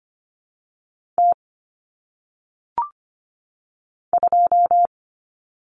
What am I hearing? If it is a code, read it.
Morse code, heard as a keyed tone, about 25 wpm, 700 Hz. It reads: T 2
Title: Common Cut Numbers - 1-Digit 25wpm